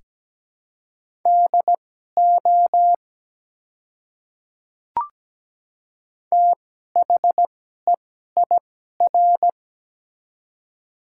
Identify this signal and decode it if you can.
Morse code, heard as a keyed tone, about 17 wpm, spaced as for 12 wpm, 700 Hz. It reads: DO THEIR
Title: 2x Character Spacing - Top 100 Words - 17wpm